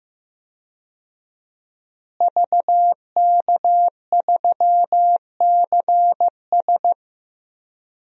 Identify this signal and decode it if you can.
Morse code, heard as a keyed tone, about 15 wpm, 700 Hz. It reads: VK3CS
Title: DXCC-100 Call Signs - 5 Characters - Repeated 2x - 15wpm